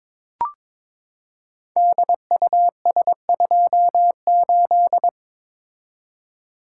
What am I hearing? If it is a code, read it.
Morse code, heard as a keyed tone, about 22 wpm, 700 Hz. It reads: DUS28
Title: Code Groups - 5 Letter:Numbers - 22wpm